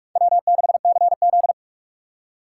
Morse code, heard as a keyed tone, about 41 wpm, 700 Hz. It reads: W6CZ
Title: USA Call Signs - 4 Characters - 4x Character Spacing - 45wpm